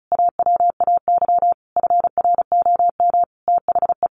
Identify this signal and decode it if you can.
Morse code, heard as a keyed tone, about 34 wpm, 700 Hz. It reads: AWAY FROM THE
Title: Sentences from Top 1500 Words 35wpm